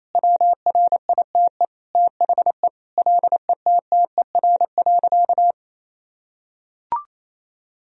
Morse code, heard as a keyed tone, about 28 wpm, 700 Hz. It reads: WRITE THE LETTER.
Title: Sentences from Top 1000 Words - Review Part 3 - 28wpm